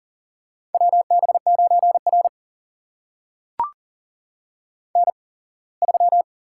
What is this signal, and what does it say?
Morse code, heard as a keyed tone, about 40 wpm, spaced as for 24 wpm, 700 Hz. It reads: WB9R N3
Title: USA Call Signs - 8x Character Spacing - 40wpm